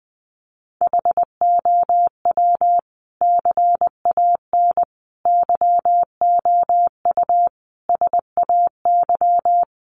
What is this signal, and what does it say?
Morse code, heard as a keyed tone, about 20 wpm, 700 Hz. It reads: HOW CAN YOU SAY
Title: Sentences from Top 200 Words 20wpm